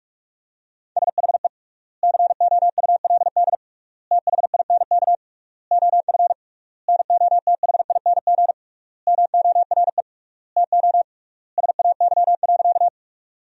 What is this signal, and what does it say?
Morse code, heard as a keyed tone, about 41 wpm, 700 Hz. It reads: SHE COULD THINK OF NOTHING MORE TO SAY.
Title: 3x Word-Spacing - Sentences from Top 500 Words - 45wpm